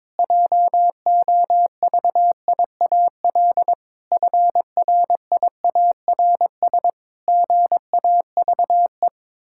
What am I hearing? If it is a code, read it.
Morse code, heard as a keyed tone, about 22 wpm, 700 Hz. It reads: JOVIAL FRIARS GAVE
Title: Pangram Challenge v2 - 22wpm